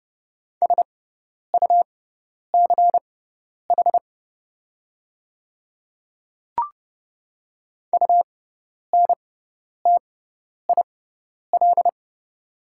Morse code, heard as a keyed tone, about 30 wpm, spaced as for 10 wpm, 700 Hz. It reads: SUCH UNTIL
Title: 6x Character Spacing - Top 300 Words - 30wpm